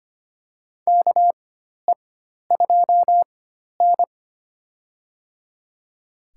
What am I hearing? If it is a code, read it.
Morse code, heard as a keyed tone, about 25 wpm, spaced as for 12 wpm, 700 Hz. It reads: KE2N
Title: USA Call Signs - 4 Characters - 4x Character Spacing - 25wpm